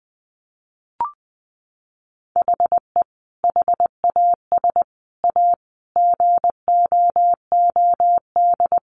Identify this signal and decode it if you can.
Morse code, heard as a keyed tone, about 20 wpm, 700 Hz. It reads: HE HAS A GOOD
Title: Sentences from Top 500 Words 20wpm